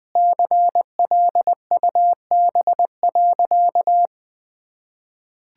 Morse code, heard as a keyed tone, about 20 wpm, 700 Hz. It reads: CLUB.
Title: Sentences with 5-Word Phrases - 20wpm